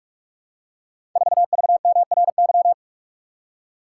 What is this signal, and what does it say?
Morse code, heard as a keyed tone, about 44 wpm, 700 Hz. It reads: 4VMRY
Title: Code Groups - 5 Letter:Numbers - 45wpm